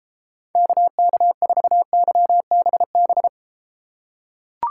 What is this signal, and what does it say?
Morse code, heard as a keyed tone, about 32 wpm, 700 Hz. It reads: KK4YBB